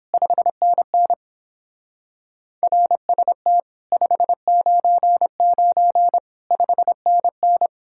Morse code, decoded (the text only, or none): 5NN RST 599 5NN